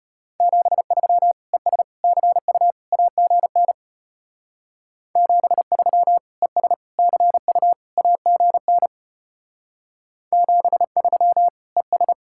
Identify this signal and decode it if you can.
Morse code, heard as a keyed tone, about 31 wpm, 700 Hz. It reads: 73 ES CU AGN 73 ES CU AGN 73 ES
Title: Speed-Racing Common QSO Phrases - 25wpm